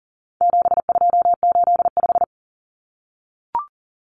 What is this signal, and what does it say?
Morse code, heard as a keyed tone, about 40 wpm, 700 Hz. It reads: 7285
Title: Common Cut Numbers - 4-Digit 40wpm